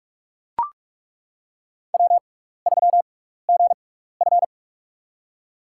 Morse code, heard as a keyed tone, about 41 wpm, spaced as for 15 wpm, 700 Hz. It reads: W3GF
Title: USA Call Signs - 6x Character Spacing - 45wpm